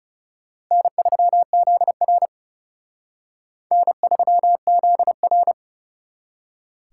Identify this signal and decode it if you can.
Morse code, heard as a keyed tone, about 38 wpm, 700 Hz. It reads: N3ZR N3ZR